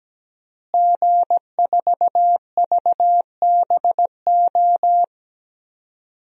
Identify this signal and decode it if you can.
Morse code, heard as a keyed tone, about 17 wpm, 700 Hz. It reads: G4VBO
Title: DXCC-100 Call Signs - 5 Characters - Repeated 3x - 17wpm